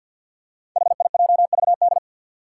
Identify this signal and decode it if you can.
Morse code, heard as a keyed tone, about 47 wpm, 700 Hz. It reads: HIJVD
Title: Five Challenging Letter-Number Combinations 50wpm